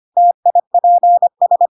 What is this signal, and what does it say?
Morse code, heard as a keyed tone, about 25 wpm, 700 Hz. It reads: TIPS
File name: Pinocchio / Pinocchio ch5 25 25 ik1hge